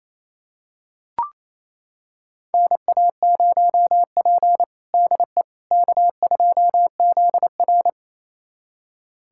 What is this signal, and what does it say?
Morse code, heard as a keyed tone, about 28 wpm, 700 Hz. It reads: NA0P DE K2ZR
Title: USA Callsign DE Callsign - 4 Characters - 28wpm